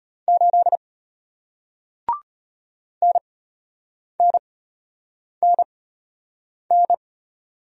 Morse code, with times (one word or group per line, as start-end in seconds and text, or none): 0.3-0.8 8
3.0-6.9 NNNN